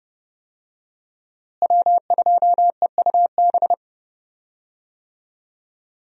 Morse code, heard as a keyed tone, about 31 wpm, 700 Hz. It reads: W2EUB
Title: USA Call Signs - 5 Characters - Repeated 2x - 30wpm